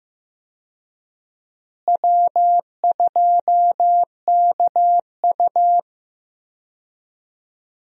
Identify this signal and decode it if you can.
Morse code, heard as a keyed tone, about 15 wpm, 700 Hz. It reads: W2KU
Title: USA Call Signs - 4 Characters - Repeated 3x - 15wpm